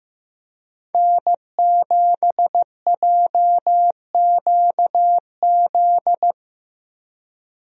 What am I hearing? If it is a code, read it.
Morse code, heard as a keyed tone, about 15 wpm, 700 Hz. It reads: N7JQZ
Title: USA Call Signs - 5 Characters - Repeated 3x - 15wpm